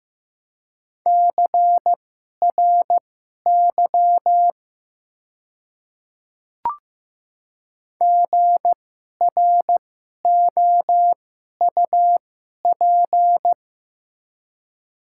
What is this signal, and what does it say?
Morse code, heard as a keyed tone, about 15 wpm, spaced as for 11 wpm, 700 Hz. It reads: CRY GROUP
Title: 2x Character Spacing - Top 500 Words - 15wpm